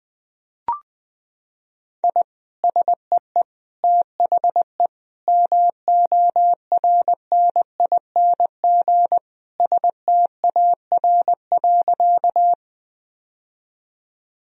Morse code, heard as a keyed tone, about 19 wpm, 700 Hz. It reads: I SEE THE MORNING STAR.
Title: Sentences from Top 1000 Words - Review Part 2 - 20wpm